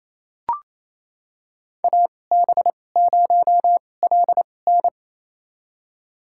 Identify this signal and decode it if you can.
Morse code, heard as a keyed tone, about 28 wpm, spaced as for 20 wpm, 700 Hz. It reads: AB0LN
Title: USA Call Signs - 5 Characters - 2x Character Spacing - 28wpm